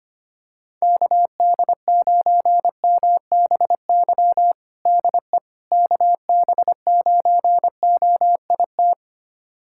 Morse code, heard as a keyed tone, about 25 wpm, 700 Hz. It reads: KD9MBY DE KB9OIT
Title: USA Callsign DE Callsign 25wpm